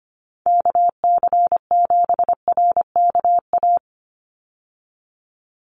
Morse code, heard as a keyed tone, about 25 wpm, 700 Hz. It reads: KC7RKA
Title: USA Call Signs 25wpm